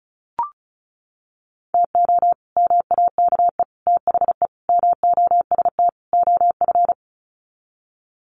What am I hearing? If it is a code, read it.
Morse code, heard as a keyed tone, about 34 wpm, 700 Hz. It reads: TO MAKE THE MOST OF
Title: Sets of 5 Words Encore2 35wpm